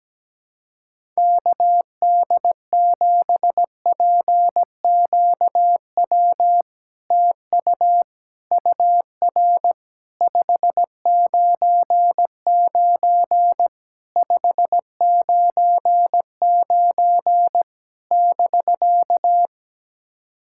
Morse code, heard as a keyed tone, about 17 wpm, 700 Hz. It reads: KD7PQW TU UR 599 599 <BK>
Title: SOTA QSO Practice 17wpm